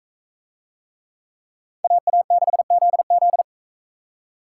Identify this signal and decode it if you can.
Morse code, heard as a keyed tone, about 41 wpm, 700 Hz. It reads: AA6ZZ